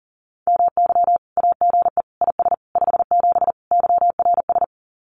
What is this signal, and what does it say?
Morse code, heard as a keyed tone, about 40 wpm, 700 Hz. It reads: MY AGE IS 57 YRS
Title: Common QSO Phrases Encore 40wpm